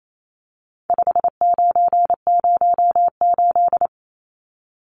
Morse code, heard as a keyed tone, about 28 wpm, 700 Hz. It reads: 5908
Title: Numbers - 4-Digit 28wpm